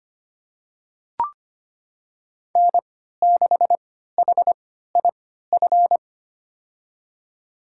Morse code, heard as a keyed tone, about 25 wpm, spaced as for 14 wpm, 700 Hz. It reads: N6HIF
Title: USA Call Signs - 5 Characters - 3x Character Spacing - 25wpm